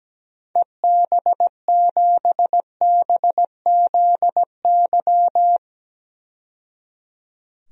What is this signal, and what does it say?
Morse code, heard as a keyed tone, about 17 wpm, 700 Hz. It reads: EB7BZY
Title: DXCC-100 Call Signs - 6 Characters - 17wpm